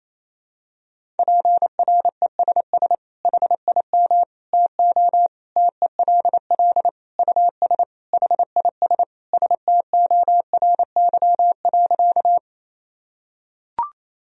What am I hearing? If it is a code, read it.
Morse code, heard as a keyed tone, about 28 wpm, 700 Hz. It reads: PRESS HIM TO TELL US HIS STORY.
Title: Sentences from Top 1000 Words - Review Part 1 - 28wpm